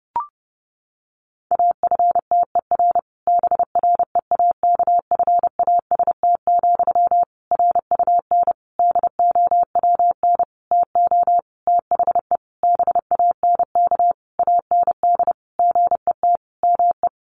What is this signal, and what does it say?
Morse code, heard as a keyed tone, about 30 wpm, 700 Hz. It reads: AFTER BREAKFAST, RUN DOWN TO THE BANK AND GET ME
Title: Sentences from Top 1500 Words 30wpm